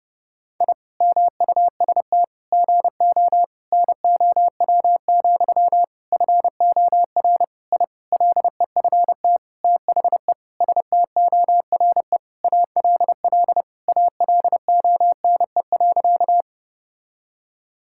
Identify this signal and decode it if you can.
Morse code, heard as a keyed tone, about 30 wpm, 700 Hz. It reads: I MUST GO NOW, FOR I LEFT THE STORE ALL ALONE.
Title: Sentences from Top 1000 Words - Review Part 5 - 30wpm